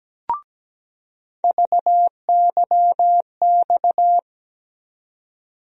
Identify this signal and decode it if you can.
Morse code, heard as a keyed tone, about 17 wpm, 700 Hz. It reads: VYX
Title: USA Call Sign Postfixes - 3 Letters 17wpm